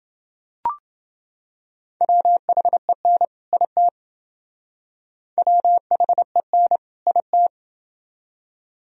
Morse code, heard as a keyed tone, about 28 wpm, 700 Hz. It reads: WHEN IT WHEN IT